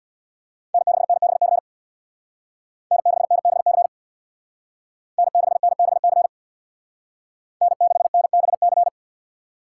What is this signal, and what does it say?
Morse code, heard as a keyed tone, about 65 wpm, 700 Hz. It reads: N6NBC N6NBC N6NBC N6NBC